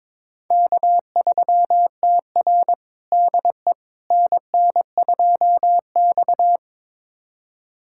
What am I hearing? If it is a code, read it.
Morse code, heard as a keyed tone, about 22 wpm, 700 Hz. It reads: K3TR DE NN2X
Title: USA Callsign DE Callsign - 4 Characters - 22wpm